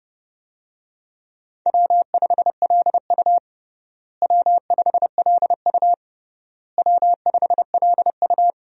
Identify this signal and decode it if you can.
Morse code, heard as a keyed tone, about 30 wpm, 700 Hz. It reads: W5LU W5LU W5LU
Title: USA Call Signs - 4 Characters - Repeated 3x - 30wpm